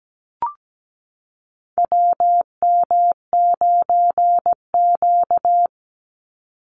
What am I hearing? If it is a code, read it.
Morse code, heard as a keyed tone, about 17 wpm, 700 Hz. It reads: WM9Q
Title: Call Signs Repeated 3x - 17wpm